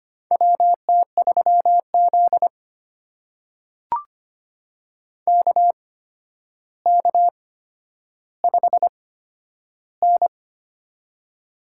Morse code, heard as a keyed tone, about 25 wpm, spaced as for 11 wpm, 700 Hz. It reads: WT3Z KK5N
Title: USA Call Signs - 4 Characters - 8x Character Spacing - 25wpm